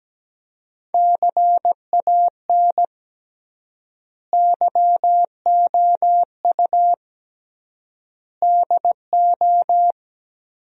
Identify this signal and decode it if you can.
Morse code, heard as a keyed tone, about 17 wpm, 700 Hz. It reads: CAN YOU DO
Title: 3x Word-Spacing - Sentences from Top 100 Words - 17wpm